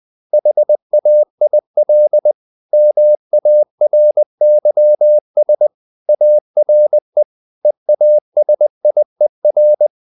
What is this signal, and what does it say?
Morse code, heard as a keyed tone, about 20 wpm, 600 Hz. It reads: HAIL MARYS ARE EASIER
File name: the old man and the sea / the old man and the sea part 1 20wpm